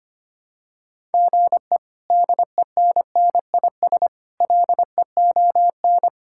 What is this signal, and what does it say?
Morse code, heard as a keyed tone, about 25 wpm, 700 Hz. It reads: GE DENNIS LEON
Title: K1USN SST QSO Practice - 25wpm